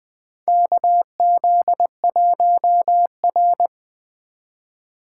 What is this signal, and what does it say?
Morse code, heard as a keyed tone, about 20 wpm, 700 Hz. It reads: KZ1R
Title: USA Call Signs - 4 Characters - 2x Character Spacing - 20wpm